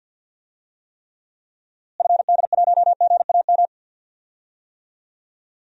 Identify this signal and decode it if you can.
Morse code, heard as a keyed tone, about 52 wpm, 700 Hz. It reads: FD1GAM
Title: DXCC-100 Call Signs - 6 Characters - Repeated 2x - 50wpm